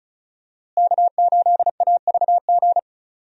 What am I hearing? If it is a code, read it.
Morse code, heard as a keyed tone, about 34 wpm, 700 Hz. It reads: K8AVG
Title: USA Call Signs - 5 Characters - Repeated 3x - 35wpm